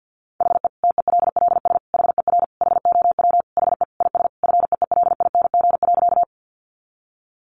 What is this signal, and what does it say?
Morse code, heard as a keyed tone, about 50 wpm, 700 Hz. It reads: HE TELLS HER HOW HE IS FEELING.
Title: Additional Easy Phrases 50wpm